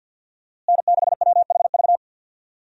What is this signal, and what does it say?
Morse code, heard as a keyed tone, about 52 wpm, 700 Hz. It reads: N6WHV
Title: USA Call Signs - 5 Characters - Repeated 2x - 50wpm